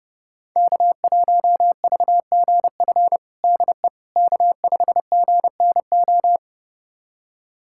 Morse code, heard as a keyed tone, about 30 wpm, 700 Hz. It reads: K1VGF DE K5GNO
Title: USA Callsign DE Callsign - 5 Characters - 30wpm